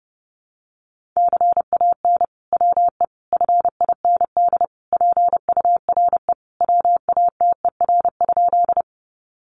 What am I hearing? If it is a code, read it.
Morse code, heard as a keyed tone, about 30 wpm, 700 Hz. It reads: CAN WE FIND PURE WATER?
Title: Sentences from Top 1500 Words 30wpm